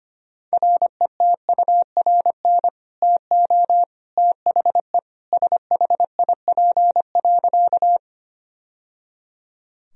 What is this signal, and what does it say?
Morse code, heard as a keyed tone, about 25 wpm, 700 Hz. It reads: RETURN TO THE SHIP.
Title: Sentences from Top 1000 Words - Review Part 2 - 25wpm